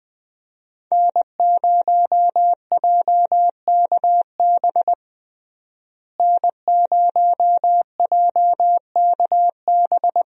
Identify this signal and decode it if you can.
Morse code, heard as a keyed tone, about 20 wpm, 700 Hz. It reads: N0JKB N0JKB
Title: USA Call Signs - 5 Characters - Repeated 2x - 20wpm